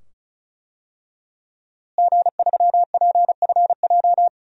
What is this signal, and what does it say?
Morse code, heard as a keyed tone, about 36 wpm, 700 Hz. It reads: G3PFJ
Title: DXCC-100 Call Signs - 5 Characters - Repeated 2x - 35wpm